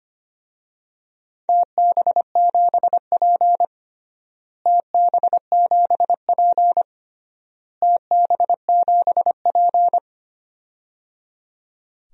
Morse code, heard as a keyed tone, about 25 wpm, 700 Hz. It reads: TB7P TB7P TB7P
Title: DXCC-100 Call Signs - 4 Characters - Repeated 3x - 25wpm